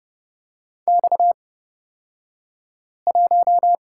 Morse code, heard as a keyed tone, about 30 wpm, spaced as for 5 wpm, 700 Hz. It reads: X1
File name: Lesson 38d - Letter q - Two Characters 30wpm